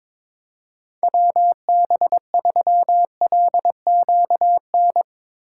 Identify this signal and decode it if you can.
Morse code, heard as a keyed tone, about 22 wpm, 700 Hz. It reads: WB3LQN
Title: Speed Racing - USA 2x3 Call Signs 22wpm